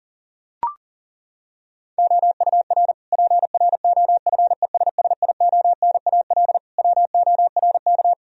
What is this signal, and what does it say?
Morse code, heard as a keyed tone, about 40 wpm, 700 Hz. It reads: OUR PROFESSIONAL WORK
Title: Sentences from Top 1000 Words - Review Part 4 - 40wpm